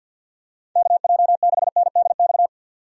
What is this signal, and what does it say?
Morse code, heard as a keyed tone, about 49 wpm, 700 Hz. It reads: KJ6NDX
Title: USA Call Signs - 6 Characters - 50wpm